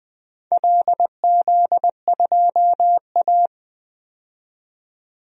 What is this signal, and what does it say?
Morse code, heard as a keyed tone, about 20 wpm, 700 Hz. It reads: LZ2A
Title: DXCC-100 Call Signs - 4 Characters - 20wpm